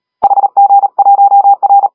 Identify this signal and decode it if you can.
Morse code, heard as a keyed tone, about 37 wpm, 800 Hz. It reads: HG1R
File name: HG1R CW SSB MIX